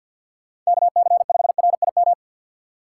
Morse code, heard as a keyed tone, about 52 wpm, 700 Hz. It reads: KC5RIM
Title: USA Call Signs - 6 Characters - 6x Character Spacing - 50wpm